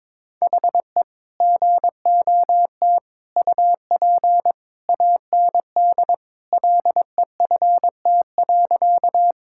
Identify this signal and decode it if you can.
Morse code, heard as a keyed tone, about 22 wpm, 700 Hz. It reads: HE GOT UP AND LEFT.
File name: practice / 22wpm / Sentences with 2-Word Phrases - 22wpm